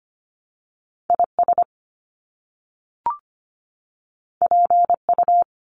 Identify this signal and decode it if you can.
Morse code, heard as a keyed tone, about 25 wpm, 700 Hz. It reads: IS PU
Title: Top Two Letter Combinations in Words 25wpm